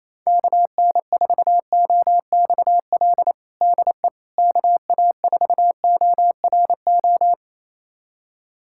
Morse code, heard as a keyed tone, about 28 wpm, 700 Hz. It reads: KN4OXL DE KA4ORO
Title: USA Callsign DE Callsign 28wpm